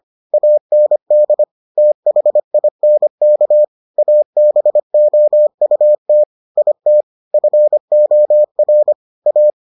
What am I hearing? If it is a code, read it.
Morse code, heard as a keyed tone, about 25 wpm, 600 Hz. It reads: AND THINK ABOUT IT FOR A